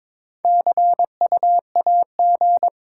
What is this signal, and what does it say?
Morse code, heard as a keyed tone, about 22 wpm, 700 Hz. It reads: CU AG
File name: Top 20 QSO Elements - 8x Character Spacing - 22wpm